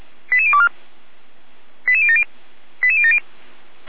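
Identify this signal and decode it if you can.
Five-tone selective call (ZVEI), heard as quick runs of five tones, tones of about 70 ms. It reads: 88024 88070 88070
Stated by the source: ZVEI five-tone selective call received by SDR.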